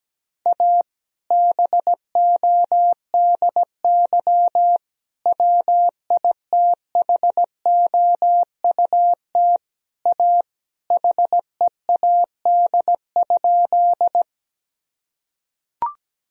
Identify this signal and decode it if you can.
Morse code, heard as a keyed tone, about 17 wpm, 700 Hz. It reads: A BODY WITHOUT A HEAD?
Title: Sentences from Top 1000 Words - Review Part 3 - 17wpm